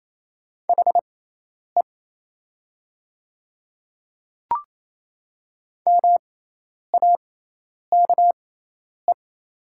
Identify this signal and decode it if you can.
Morse code, heard as a keyed tone, about 28 wpm, spaced as for 10 wpm, 700 Hz. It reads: HE MAKE
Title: 6x Character Spacing - Top 200 Words - 28wpm